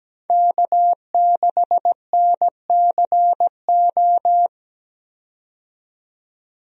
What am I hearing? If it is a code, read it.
Morse code, heard as a keyed tone, about 17 wpm, 700 Hz. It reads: K6NCO
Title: USA Call Signs - 5 Characters - 17wpm